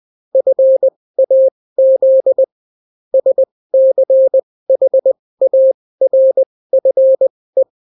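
Morse code, heard as a keyed tone, about 20 wpm, spaced as for 16 wpm, 530 Hz. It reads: FAZ SCHARFE
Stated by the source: Daily Morse weather and top stories at 16 wpm, clean without QRN.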